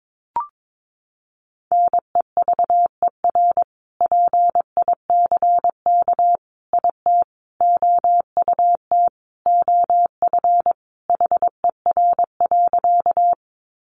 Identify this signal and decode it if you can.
Morse code, heard as a keyed tone, about 22 wpm, 700 Hz. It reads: NEVER PICK IT OUT OF HER.
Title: Sentences from Top 1000 Words 22wpm